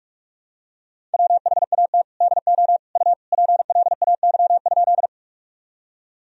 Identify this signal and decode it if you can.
Morse code, heard as a keyed tone, about 42 wpm, 700 Hz. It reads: WHAT DO U PLAY?
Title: Sentences from Top 200 Words and Abbreviations 45wpm